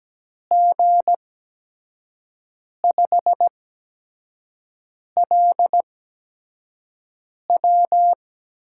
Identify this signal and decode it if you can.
Morse code, heard as a keyed tone, about 17 wpm, spaced as for 5 wpm, 700 Hz. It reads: G5LW
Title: USA Call Signs - 5 Characters - 8x Character Spacing - 17wpm